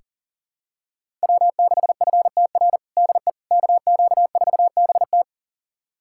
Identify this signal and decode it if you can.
Morse code, heard as a keyed tone, about 37 wpm, 700 Hz. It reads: W6FTR DE KQ4BT
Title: USA Callsign DE Callsign - 5 Characters - 40wpm